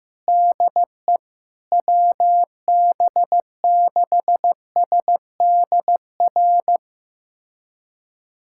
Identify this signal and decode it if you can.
Morse code, heard as a keyed tone, about 15 wpm, 700 Hz. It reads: DE WB6SDR
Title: USA Callsign DE Callsign - 6 Characters - 15wpm